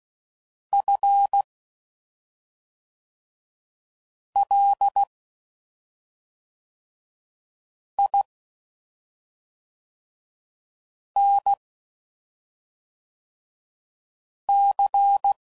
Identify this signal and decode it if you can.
Morse code, heard as a keyed tone, about 16 wpm, 800 Hz. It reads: F L I N C
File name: Suffixes / Special ING's / SlowToFaster ING's